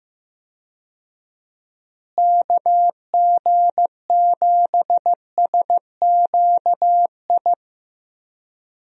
Morse code, heard as a keyed tone, about 15 wpm, 700 Hz. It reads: KG7SQI
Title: USA Call Signs - 6 Characters - Repeated 3x - 15wpm